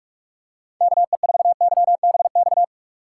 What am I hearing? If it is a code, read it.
Morse code, heard as a keyed tone, about 46 wpm, 700 Hz. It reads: KE4YBX
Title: USA Call Signs - 6 Characters - 6x Character Spacing - 45wpm